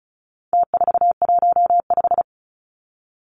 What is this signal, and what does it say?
Morse code, heard as a keyed tone, about 34 wpm, 700 Hz. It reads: T415
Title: Numbers - Contest Style - 4-Digit 35wpm